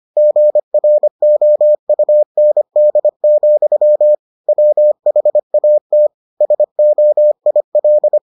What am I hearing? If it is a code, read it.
Morse code, heard as a keyed tone, about 25 wpm, 600 Hz. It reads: GROUND, WHAT SOIL